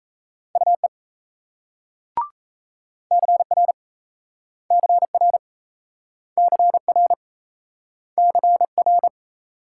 Morse code, heard as a keyed tone, about 35 wpm, 700 Hz. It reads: UE CR CR CR CR
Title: Speed-Racing Multi-letter Phonemes - 28wpm